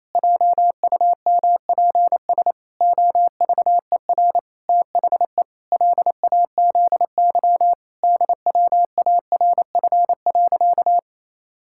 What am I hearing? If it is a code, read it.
Morse code, heard as a keyed tone, about 28 wpm, 700 Hz. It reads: JUMPS OVER THE LAZY DWARF.
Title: Pangram Challenge v2 - 28wpm